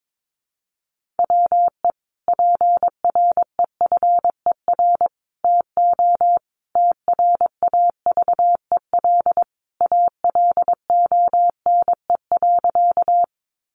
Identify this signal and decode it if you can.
Morse code, heard as a keyed tone, about 22 wpm, 700 Hz. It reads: WE PREFER TO TRAVEL ALONE.
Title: Sentences from Top 1100 Words 22wpm